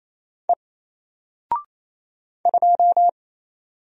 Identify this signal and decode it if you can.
Morse code, heard as a keyed tone, about 27 wpm, 700 Hz. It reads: E 2
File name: Single-Letter-Number-28wpm